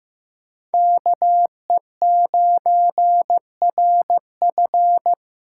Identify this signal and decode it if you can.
Morse code, heard as a keyed tone, about 15 wpm, 700 Hz. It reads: KE9RF
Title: USA Call Signs - 5 Characters - 2x Character Spacing - 15wpm